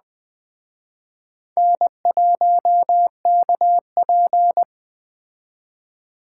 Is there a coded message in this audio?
Yes: Morse code from 1.6 to 4.6 s, heard as a keyed tone, about 21 wpm, 700 Hz. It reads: N1KP